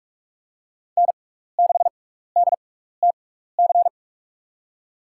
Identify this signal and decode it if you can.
Morse code, heard as a keyed tone, about 42 wpm, spaced as for 15 wpm, 690 Hz. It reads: N6DTC
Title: USA Call Signs - 5 Characters - 6x Character Spacing - 45wpm